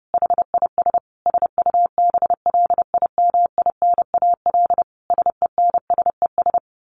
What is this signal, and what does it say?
Morse code, heard as a keyed tone, about 30 wpm, 700 Hz. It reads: HIS SUBLIMINAL SENSES
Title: A Sound Of Thunder 30wpm